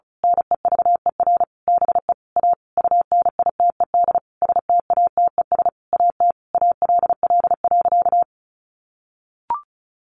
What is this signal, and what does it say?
Morse code, heard as a keyed tone, about 34 wpm, 700 Hz. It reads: NEVER BE A UNITED STATES AT ALL.
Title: A Sound Of Thunder 35wpm